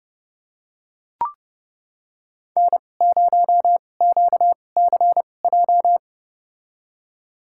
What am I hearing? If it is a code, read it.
Morse code, heard as a keyed tone, about 30 wpm, spaced as for 22 wpm, 700 Hz. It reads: N0QCJ
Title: USA Call Signs - 5 Characters - 2x Character Spacing - 30wpm